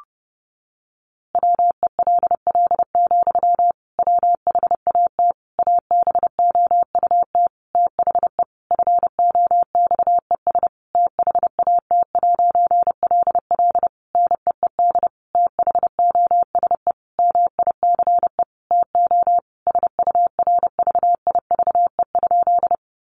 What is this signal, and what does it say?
Morse code, heard as a keyed tone, about 28 wpm, 700 Hz. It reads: WELL, WHAT ABOUT THE FOXES THAT'LL NEED THOSE MICE TO SURVIVE?
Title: A Sound Of Thunder 30wpm